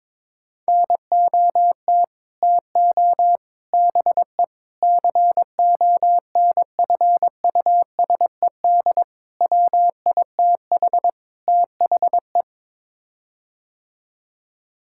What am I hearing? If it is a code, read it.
Morse code, heard as a keyed tone, about 22 wpm, 700 Hz. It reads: NOT TO BE CONFUSED WITH THE
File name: practice / 22wpm / Sets of 6 Words - 22wpm